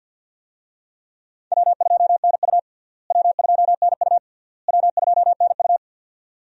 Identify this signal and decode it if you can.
Morse code, heard as a keyed tone, about 47 wpm, 700 Hz. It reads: W2NU W2NU W2NU
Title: USA Call Signs - 4 Characters - Repeated 3x - 50wpm